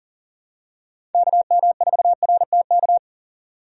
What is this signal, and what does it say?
Morse code, heard as a keyed tone, about 40 wpm, 700 Hz. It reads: KM4RTK
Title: USA Call Signs - 6 Characters - 40wpm